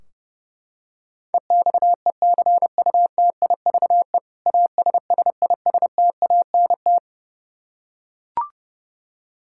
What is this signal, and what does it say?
Morse code, heard as a keyed tone, about 30 wpm, 700 Hz. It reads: EXECUTIVE ASSISTANT
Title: Practice QSO - Exchange 3 - More Info - 30wpm